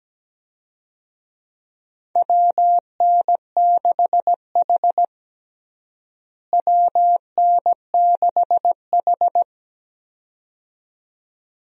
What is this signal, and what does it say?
Morse code, heard as a keyed tone, about 17 wpm, 700 Hz. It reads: WN6H WN6H
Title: USA Call Signs - 4 Characters - Repeated 2x - 17wpm